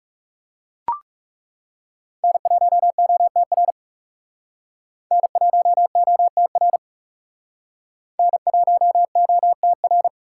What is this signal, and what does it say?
Morse code, heard as a keyed tone, about 38 wpm, 700 Hz. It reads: N1OTR N1OTR N1OTR